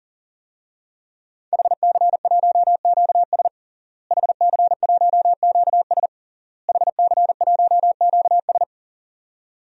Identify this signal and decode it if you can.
Morse code, heard as a keyed tone, about 40 wpm, 700 Hz. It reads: HC1QS HC1QS HC1QS
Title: DXCC-100 Call Signs - 5 Characters - Repeated 3x - 40wpm